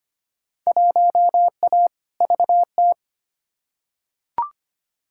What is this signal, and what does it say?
Morse code, heard as a keyed tone, about 25 wpm, 700 Hz. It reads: 1A VT
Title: Field Day Exchanges Encore - 25wpm